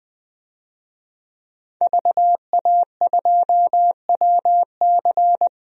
Morse code, heard as a keyed tone, about 21 wpm, 700 Hz. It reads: VA2WC
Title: DXCC-100 Call Signs - 5 Characters - Repeated 3x - 20wpm